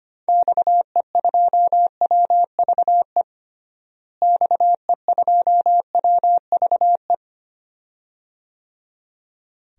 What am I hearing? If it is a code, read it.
Morse code, heard as a keyed tone, about 25 wpm, 700 Hz. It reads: XE2WVE XE2WVE
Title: DXCC-100 Call Signs - 6 Characters - Repeated 2x - 25wpm